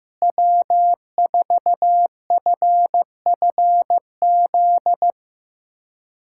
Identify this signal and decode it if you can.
Morse code, heard as a keyed tone, about 15 wpm, 700 Hz. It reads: W4FFZ
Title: USA Callsign DE Callsign - 5 Characters - 15wpm